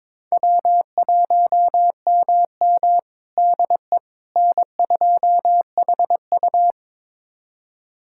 Morse code, heard as a keyed tone, about 22 wpm, 700 Hz. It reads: W1MM DE N2HU
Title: USA Callsign DE Callsign - 4 Characters - 22wpm